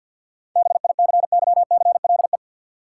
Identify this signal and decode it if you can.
Morse code, heard as a keyed tone, about 51 wpm, 700 Hz. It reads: BICYCLE
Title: Seven Letter Words - 50wpm